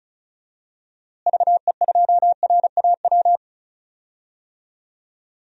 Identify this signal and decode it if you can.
Morse code, heard as a keyed tone, about 34 wpm, 700 Hz. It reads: VE2RAW
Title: DXCC-100 Call Signs - 6 Characters - Repeated 3x - 35wpm